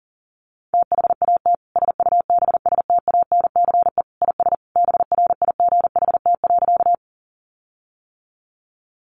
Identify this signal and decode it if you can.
Morse code, heard as a keyed tone, about 40 wpm, 700 Hz. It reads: THAT SUBSTANCE IS BRIGHT.
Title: Sentences from Top 1500 Words 40wpm